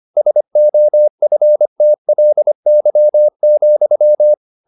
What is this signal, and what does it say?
Morse code, heard as a keyed tone, about 25 wpm, 600 Hz. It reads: SOFTLY,
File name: the old man and the sea part 1 25wpm